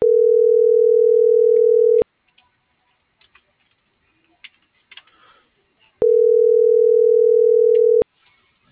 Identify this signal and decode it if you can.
Ringback tone (North America), 440 + 480 Hz.